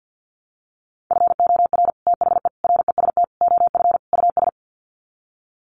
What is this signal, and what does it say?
Morse code, heard as a keyed tone, about 50 wpm, 700 Hz. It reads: FOR THE REST OF US
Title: Sets of 5 Words Encore 50wpm